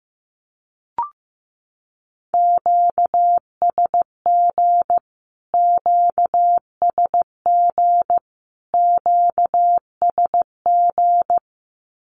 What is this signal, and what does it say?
Morse code, heard as a keyed tone, about 15 wpm, 700 Hz. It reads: QSG QSG QSG
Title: Q-Codes Repeated 3x 15wpm